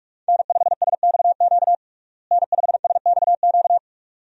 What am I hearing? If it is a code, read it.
Morse code, heard as a keyed tone, about 46 wpm, 700 Hz. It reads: N5SXQ N5SXQ
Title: USA Call Signs - 5 Characters - Repeated 2x - 45wpm